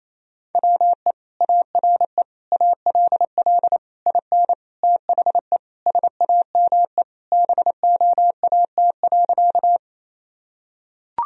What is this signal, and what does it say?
Morse code, heard as a keyed tone, about 28 wpm, 700 Hz. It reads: WE ARE ALL IN THE SAME BOAT.
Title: Sentences from Top 1000 Words - Review Part 3 - 28wpm